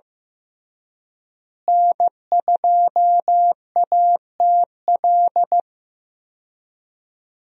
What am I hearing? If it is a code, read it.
Morse code, heard as a keyed tone, about 15 wpm, 700 Hz. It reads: N2ATL
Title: USA Call Signs - 5 Characters - Repeated 3x - 15wpm